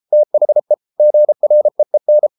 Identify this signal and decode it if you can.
Morse code, heard as a keyed tone, about 33 wpm, 600 Hz. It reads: THE GREEN